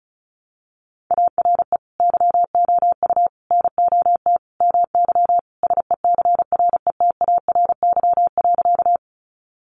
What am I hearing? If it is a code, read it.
Morse code, heard as a keyed tone, about 34 wpm, 700 Hz. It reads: ARE YOU NOT MY SECRETARY.
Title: Sentences from Top 1300 Words 35wpm